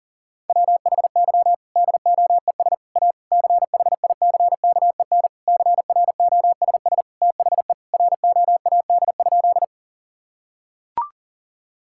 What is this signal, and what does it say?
Morse code, heard as a keyed tone, about 40 wpm, 700 Hz. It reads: WHY DOES A CHICKEN CROSS THE ROAD?
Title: Sentences from Top 1000 Words - Review Part 5 - 40wpm